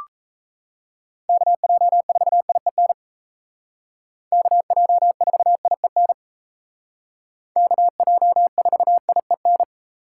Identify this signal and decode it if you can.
Morse code, heard as a keyed tone, about 36 wpm, 700 Hz. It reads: KJ4IEN KJ4IEN KJ4IEN